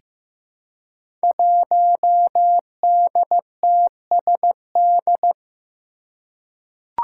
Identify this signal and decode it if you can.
Morse code, heard as a keyed tone, about 15 wpm, 700 Hz. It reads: 1DTSD
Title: Five Challenging Letter-Number Combinations 15wpm